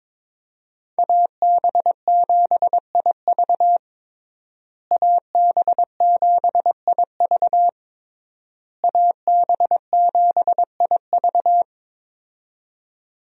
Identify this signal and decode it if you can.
Morse code, heard as a keyed tone, about 22 wpm, 700 Hz. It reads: AB7IV AB7IV AB7IV
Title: USA Call Signs - 5 Characters - Repeated 3x - 22wpm